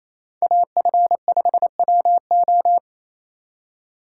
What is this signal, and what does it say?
Morse code, heard as a keyed tone, about 33 wpm, 700 Hz. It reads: AF5WO
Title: Speed Racing - USA 2x2 Call Signs 28wpm